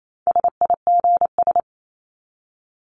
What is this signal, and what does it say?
Morse code, heard as a keyed tone, about 28 wpm, 700 Hz. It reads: SIGS
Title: Abbreviations 28wpm